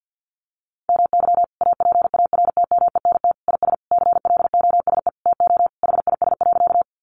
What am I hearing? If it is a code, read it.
Morse code, heard as a keyed tone, about 49 wpm, 700 Hz. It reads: MY APARTMENT IS CLOSE TO HIS.
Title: Sentences from Top 900 Words 50wpm